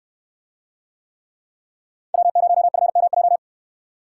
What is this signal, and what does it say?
Morse code, heard as a keyed tone, about 63 wpm, 700 Hz. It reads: W0FMJ